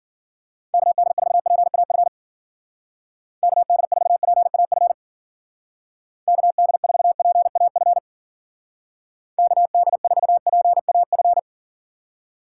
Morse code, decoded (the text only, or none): KD4PAF KD4PAF KD4PAF KD4PAF